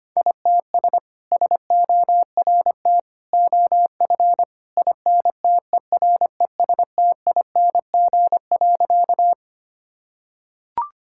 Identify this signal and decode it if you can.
Morse code, heard as a keyed tone, about 25 wpm, 700 Hz. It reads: ITS SORT OF INTERESTING.
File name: practice / 25wpm / Sentences with 2-Word Phrases - 25wpm